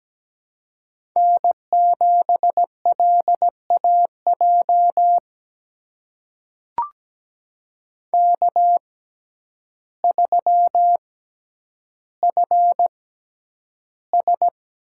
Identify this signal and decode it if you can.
Morse code, heard as a keyed tone, about 17 wpm, spaced as for 9 wpm, 700 Hz. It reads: N7LAJ K3FS
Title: USA Call Signs - 5 Characters - 6x Character Spacing - 17wpm